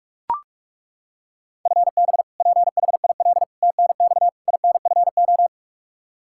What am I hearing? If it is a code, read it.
Morse code, heard as a keyed tone, about 41 wpm, 700 Hz. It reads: FB PHIL TNX INFO
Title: Practice QSO - Exchange 3 - More Info - 45wpm